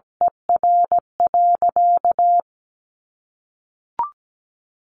Morse code, heard as a keyed tone, about 17 wpm, 700 Hz. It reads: ER.
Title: Sentences from Top 700 Words 17wpm